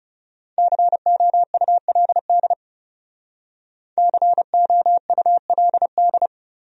Morse code, heard as a keyed tone, about 39 wpm, 700 Hz. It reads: COULD COULD